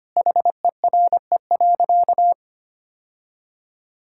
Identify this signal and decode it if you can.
Morse code, heard as a keyed tone, about 25 wpm, 700 Hz. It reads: HERE.
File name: practice / 25wpm / Sentences with 2-Word Phrases - 25wpm